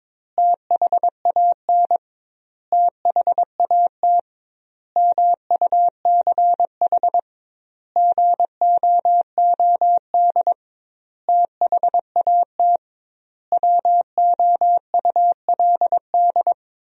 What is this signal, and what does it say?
Morse code, heard as a keyed tone, about 21 wpm, 700 Hz. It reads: THAN THAT MUCH GOOD THAT WOULD
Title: 2x Word-Spacing - Sentences from Top 200 Words - 22wpm